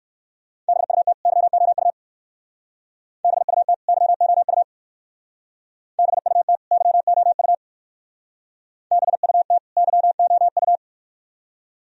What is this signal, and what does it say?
Morse code, heard as a keyed tone, about 57 wpm, 700 Hz. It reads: BUT YOU BUT YOU BUT YOU BUT YOU